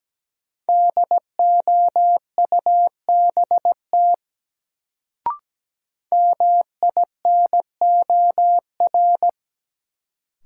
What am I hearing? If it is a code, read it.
Morse code, heard as a keyed tone, about 17 wpm, 700 Hz. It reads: DOUBT MINOR
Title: Sentences from Top 1500 Words - New Word Drill Alt - 17wpm